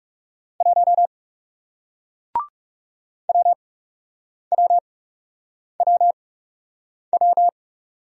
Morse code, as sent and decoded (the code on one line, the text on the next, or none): .---- / .-- .-- .-- .--
1 WWWW